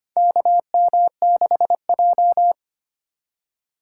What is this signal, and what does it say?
Morse code, heard as a keyed tone, about 25 wpm, 700 Hz. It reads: KM6J
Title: USA Call Signs - 4x Character Spacing - 25wpm